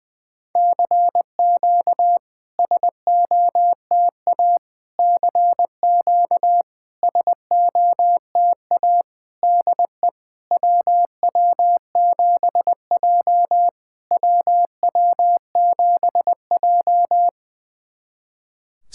Morse code, heard as a keyed tone, about 20 wpm, 700 Hz. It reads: CQ SOTA CQ SOTA DE WW7J WW7J
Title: SOTA QSO Practice 20wpm